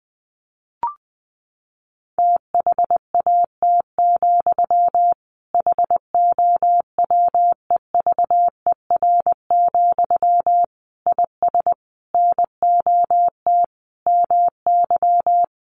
Morse code, heard as a keyed tone, about 20 wpm, 700 Hz. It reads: THAT, HOWEVER, IS NOT MY
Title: Sentences from Top 1500 Words 20wpm